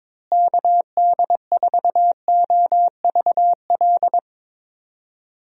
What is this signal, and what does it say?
Morse code, heard as a keyed tone, about 22 wpm, 700 Hz. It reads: KD4OVL
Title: USA Call Signs - 6 Characters - Repeated 3x - 22wpm